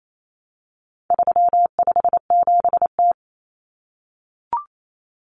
Morse code, heard as a keyed tone, about 28 wpm, 700 Hz. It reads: 357T
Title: Common Cut Numbers - 4-Digit 28wpm